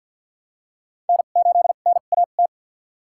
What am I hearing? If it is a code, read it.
Morse code, heard as a keyed tone, about 53 wpm, spaced as for 37 wpm, 700 Hz. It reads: N8NAT
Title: USA Call Signs - 5 Characters - 2x Character Spacing - 50wpm